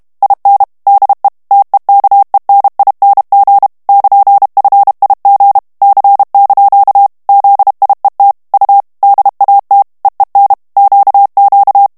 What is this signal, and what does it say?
Morse code, heard as a keyed tone, about 32 wpm, 800 Hz. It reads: IN DE TEKENING <KN>FIG C) ZIET U DAT EEN QQ
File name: HOLLANDAIS texte clair 1 vitesse 30 mots minute